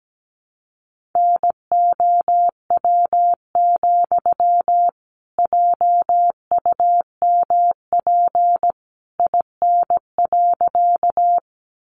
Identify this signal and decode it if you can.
Morse code, heard as a keyed tone, about 17 wpm, 700 Hz. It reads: NOW, JUMP IN.
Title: Sentences from Top 1300 Words 17wpm